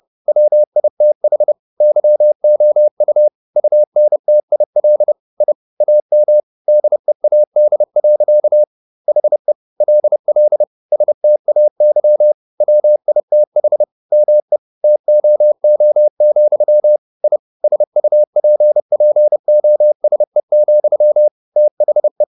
Morse code, heard as a keyed tone, about 30 wpm, 600 Hz. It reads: WITH YOU UNTIL I AM DEAD. HE LL STAY WITH ME TOO, I SUPPOSE, THE
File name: the old man and the sea / the old man and the sea 1 30wpm